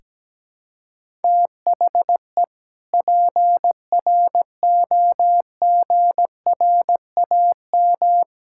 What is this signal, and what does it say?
Morse code, heard as a keyed tone, about 17 wpm, 700 Hz. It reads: THE PROGRAM
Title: Sentences from Top 1000 Words - Review Part 5 - 17wpm